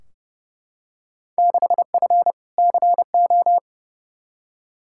Morse code, heard as a keyed tone, about 32 wpm, 700 Hz. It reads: 6F CO
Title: Field Day Exchanges Encore - 30wpm